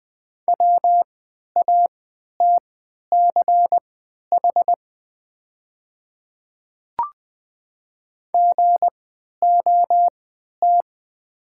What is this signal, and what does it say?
Morse code, heard as a keyed tone, about 20 wpm, spaced as for 11 wpm, 690 Hz. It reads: WATCH GOT
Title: 3x Character Spacing - Top 300 Words - 20wpm